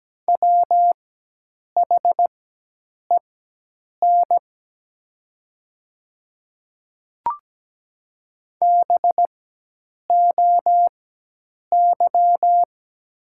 Morse code, heard as a keyed tone, about 17 wpm, spaced as for 8 wpm, 690 Hz. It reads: WHEN BOY
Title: 4x Character Spacing - Top 300 Words - 17wpm